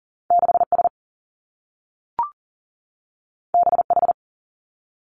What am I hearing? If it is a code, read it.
Morse code, heard as a keyed tone, about 38 wpm, 700 Hz. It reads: 6 S B H
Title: Two Challenging Letter-Number Combinations 40wpm